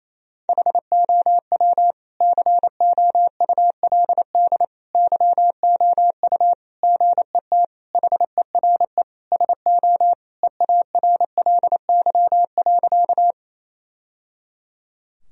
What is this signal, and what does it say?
Morse code, heard as a keyed tone, about 27 wpm, 690 Hz. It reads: HOW COULD YOU GET HERE SO EARLY.
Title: Sentences from Top 1000 Words - Review Part 2 - 28wpm